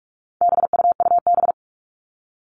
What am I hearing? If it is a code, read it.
Morse code, heard as a keyed tone, about 44 wpm, 700 Hz. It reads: BUUB
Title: All Cut Numbers - 4-Digit 45wpm